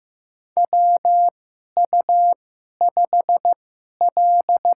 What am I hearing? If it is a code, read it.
Morse code, heard as a keyed tone, about 15 wpm, spaced as for 11 wpm, 700 Hz. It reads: WU5L
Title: USA Call Signs - 4 Characters - 2x Character Spacing - 15wpm